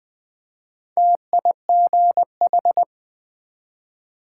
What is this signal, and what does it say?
Morse code, heard as a keyed tone, about 20 wpm, 700 Hz. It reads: TIGH
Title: Build A Word - 1 Letter At A Time - 5-Letter Words - 20wpm